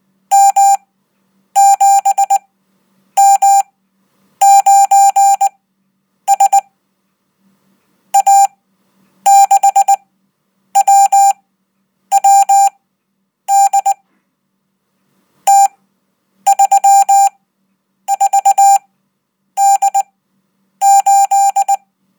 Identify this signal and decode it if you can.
Morse code, heard as a keyed tone, about 20 wpm, spaced as for 8 wpm, 790 Hz. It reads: M7M9S A6WWD T34D8
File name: Code